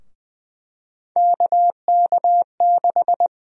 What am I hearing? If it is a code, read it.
Morse code, heard as a keyed tone, about 20 wpm, 700 Hz. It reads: KK6
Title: USA Call Sign Prefixes - 20wpm